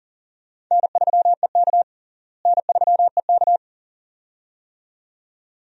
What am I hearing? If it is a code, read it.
Morse code, heard as a keyed tone, about 40 wpm, 700 Hz. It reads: N3EK N3EK
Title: USA Call Signs - 4 Characters - Repeated 2x - 40wpm